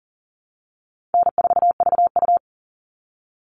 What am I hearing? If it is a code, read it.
Morse code, heard as a keyed tone, about 40 wpm, 700 Hz. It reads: N4VU
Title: All Cut Numbers - 4-Digit 40wpm